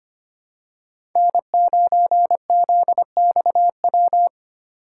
Morse code, heard as a keyed tone, about 25 wpm, 700 Hz. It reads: N9ZXW
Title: USA Call Signs - 5 Characters - 4x Character Spacing - 25wpm